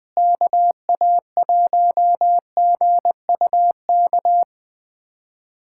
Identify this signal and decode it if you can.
Morse code, heard as a keyed tone, about 21 wpm, 700 Hz. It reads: KA1GUK
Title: USA Callsign DE Callsign - 6 Characters - 20wpm